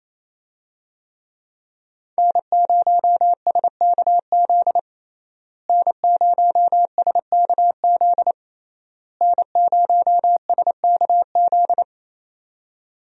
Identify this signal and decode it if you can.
Morse code, heard as a keyed tone, about 28 wpm, 700 Hz. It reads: N0SKZ N0SKZ N0SKZ
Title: USA Call Signs - 5 Characters - Repeated 3x - 28wpm